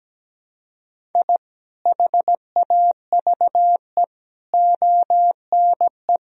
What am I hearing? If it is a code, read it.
Morse code, heard as a keyed tone, about 17 wpm, 700 Hz. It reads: I HAVE ONE
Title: Sentences from Top 1000 Words - Review Part 3 - 17wpm